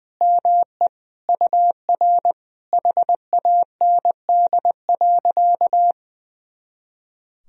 Morse code, heard as a keyed tone, about 20 wpm, 700 Hz. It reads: ME UR HAND.
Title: Sentences from Top 200 Words and Abbreviations 20wpm